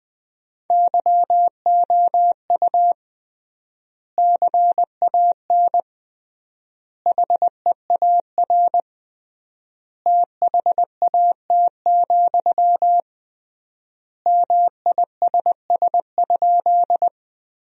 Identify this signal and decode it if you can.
Morse code, heard as a keyed tone, about 21 wpm, 700 Hz. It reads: YOU CAN HEAR THAT, MISS?
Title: 3x Word-Spacing - Sentences from Top 500 Words - 20wpm